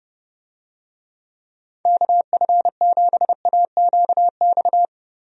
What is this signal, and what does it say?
Morse code, heard as a keyed tone, about 30 wpm, 700 Hz. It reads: KF7AQX
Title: USA Call Signs - 6 Characters - Repeated 3x - 30wpm